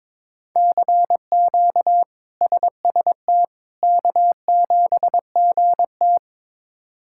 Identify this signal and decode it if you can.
Morse code, heard as a keyed tone, about 21 wpm, 700 Hz. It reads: CQ SST K7GT
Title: K1USN SST QSO Practice - 22wpm